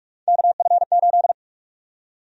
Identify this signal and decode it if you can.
Morse code, heard as a keyed tone, about 42 wpm, 700 Hz. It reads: KF8
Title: USA Call Sign Prefixes - 45wpm